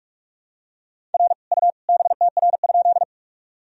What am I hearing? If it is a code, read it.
Morse code, heard as a keyed tone, about 46 wpm, 700 Hz. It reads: R U BTR?
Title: Sentences from Top 400 Words and Abbreviations 45wpm